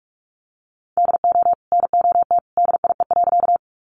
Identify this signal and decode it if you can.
Morse code, heard as a keyed tone, about 46 wpm, 700 Hz. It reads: DO NOT DIE.
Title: Sentences from Top 1300 Words 45wpm